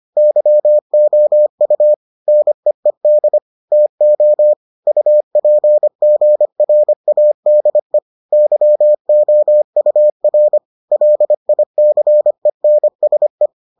Morse code, heard as a keyed tone, about 25 wpm, 600 Hz. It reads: YOU NEED TO UPGRADE YOUR LICENSE